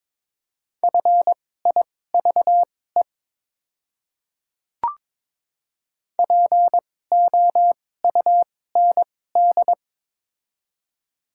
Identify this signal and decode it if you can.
Morse code, heard as a keyed tone, about 23 wpm, spaced as for 16 wpm, 700 Hz. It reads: FIVE POUND
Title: 2x Character Spacing - Top 500 Words - 22wpm